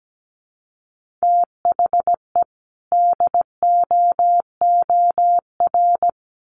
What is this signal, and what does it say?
Morse code, heard as a keyed tone, about 17 wpm, 700 Hz. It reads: THE DOOR
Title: Sentences from Top 1400 Words 17wpm